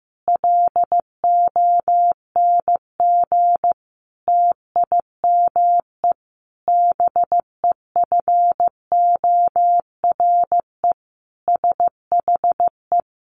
Morse code, heard as a keyed tone, about 15 wpm, 700 Hz. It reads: LONG TIME BEFORE SHE
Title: Sentences from Top 500 Words 15wpm